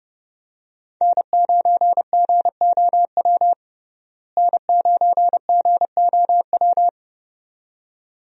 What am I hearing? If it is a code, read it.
Morse code, heard as a keyed tone, about 32 wpm, 700 Hz. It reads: N9GOW N9GOW
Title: USA Call Signs - 5 Characters - Repeated 2x - 30wpm